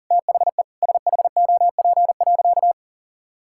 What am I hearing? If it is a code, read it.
Morse code, heard as a keyed tone, about 40 wpm, 700 Hz. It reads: THE SHOP.
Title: Sentences from Top 1600 Words 40wpm